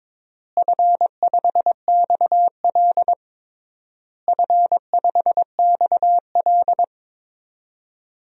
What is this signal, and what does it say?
Morse code, heard as a keyed tone, about 22 wpm, 700 Hz. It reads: F5XL F5XL
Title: DXCC-100 Call Signs - 4 Characters - Repeated 3x - 22wpm